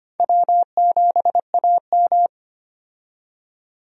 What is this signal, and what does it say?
Morse code, heard as a keyed tone, about 25 wpm, 700 Hz. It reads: W7AM
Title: USA Call Signs - 4 Characters - 25wpm